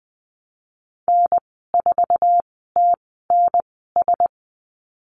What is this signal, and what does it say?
Morse code, heard as a keyed tone, about 20 wpm, spaced as for 15 wpm, 700 Hz. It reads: N4TNS
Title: 076 Lesson 13g - Letter L - Call Signs 20wpm